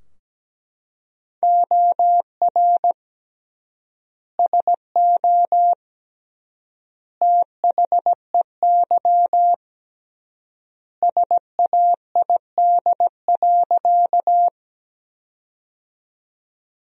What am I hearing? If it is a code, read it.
Morse code, heard as a keyed tone, about 17 wpm, 700 Hz. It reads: OR SO THEY SAID.
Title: 3x Word-Spacing - Sentences from Top 100 Words - 17wpm